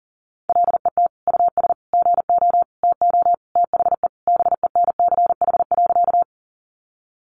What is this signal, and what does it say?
Morse code, heard as a keyed tone, about 40 wpm, 700 Hz. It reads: LET US GO TO THE BENCH.
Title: Sentences from Top 1400 Words 40wpm